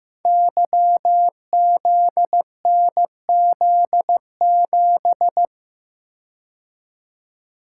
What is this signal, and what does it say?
Morse code, heard as a keyed tone, about 15 wpm, 700 Hz. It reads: YZNZ7
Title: Code Groups - 5 Letter:Numbers - 15wpm